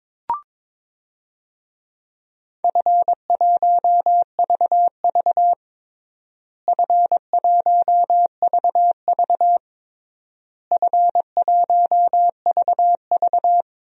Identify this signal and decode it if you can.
Morse code, heard as a keyed tone, about 22 wpm, 700 Hz. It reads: F1VV F1VV F1VV
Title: DXCC-100 Call Signs - 4 Characters - Repeated 3x - 22wpm